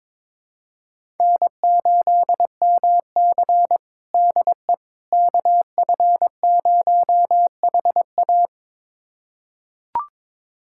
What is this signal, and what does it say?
Morse code, heard as a keyed tone, about 22 wpm, 700 Hz. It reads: N8MC DE KF0HA
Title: USA Callsign DE Callsign 22wpm